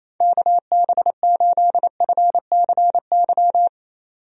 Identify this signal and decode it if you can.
Morse code, heard as a keyed tone, about 28 wpm, 700 Hz. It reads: KB8FCY
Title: USA Call Signs - 6 Characters - 28wpm